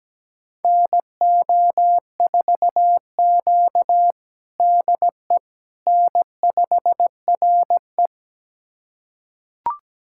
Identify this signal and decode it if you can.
Morse code, heard as a keyed tone, about 17 wpm, 700 Hz. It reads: NO4Q DE N5RE
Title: USA Callsign DE Callsign - 4 Characters - 17wpm